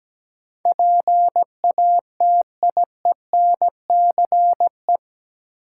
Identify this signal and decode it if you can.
Morse code, heard as a keyed tone, about 17 wpm, 700 Hz. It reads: PATIENCE
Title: Sentences from Top 1300 Words - New Word Drill - 17wpm